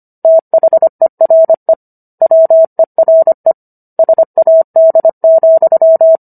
Morse code, heard as a keyed tone, about 25 wpm, 650 Hz. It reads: THERE WERE SAD,